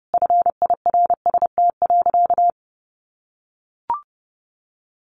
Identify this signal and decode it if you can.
Morse code, heard as a keyed tone, about 30 wpm, 700 Hz. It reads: FIRST.
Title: Sentences from Top 1400 Words 30wpm